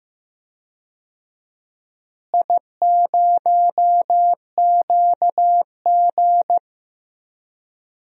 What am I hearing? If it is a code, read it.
Morse code, heard as a keyed tone, about 15 wpm, 700 Hz. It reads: I0QG
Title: DXCC-100 Call Signs - 4 Characters - Repeated 2x - 15wpm